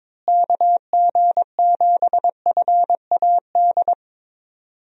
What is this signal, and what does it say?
Morse code, heard as a keyed tone, about 22 wpm, 700 Hz. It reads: KG7FAD
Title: USA Call Signs - 6 Characters - Repeated 2x - 22wpm